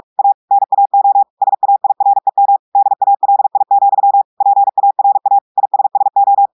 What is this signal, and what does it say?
Morse code, heard as a keyed tone, about 46 wpm, 800 Hz. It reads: A NAO SAIREM DALI, PARA ISSO
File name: audios / portugais / portugais texte en clair 2 vitesse 45 mots minutes